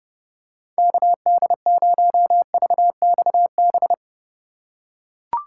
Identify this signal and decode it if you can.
Morse code, heard as a keyed tone, about 32 wpm, 700 Hz. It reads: KD0VXB
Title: USA Call Signs - 6 Characters - 4x Character Spacing - 30wpm